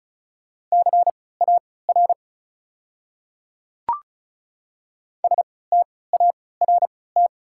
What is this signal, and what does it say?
Morse code, heard as a keyed tone, about 34 wpm, spaced as for 20 wpm, 690 Hz. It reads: CAR START
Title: 3x Character Spacing - Top 500 Words - 35wpm